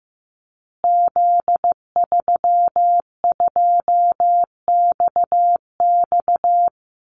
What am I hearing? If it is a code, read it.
Morse code, heard as a keyed tone, about 15 wpm, 700 Hz. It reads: Z32XX
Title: Call Signs Repeated 2x - 15wpm